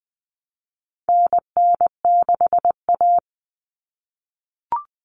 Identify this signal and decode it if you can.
Morse code, heard as a keyed tone, about 20 wpm, 700 Hz. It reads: NN6A
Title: Common Cut Numbers - 4-Digit 20wpm